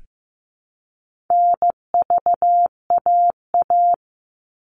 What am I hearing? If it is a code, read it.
Morse code, heard as a keyed tone, about 15 wpm, 700 Hz. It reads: NVAA
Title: All Cut Numbers - 4-Digit 15wpm